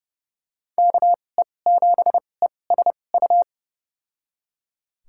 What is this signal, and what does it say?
Morse code, heard as a keyed tone, about 30 wpm, spaced as for 22 wpm, 700 Hz. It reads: KE7ESU
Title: USA Call Signs - 6 Characters - 2x Character Spacing - 30wpm